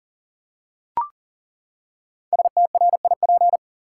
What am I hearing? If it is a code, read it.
Morse code, heard as a keyed tone, about 40 wpm, 700 Hz. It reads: STRIP
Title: Five Letter Words - 40wpm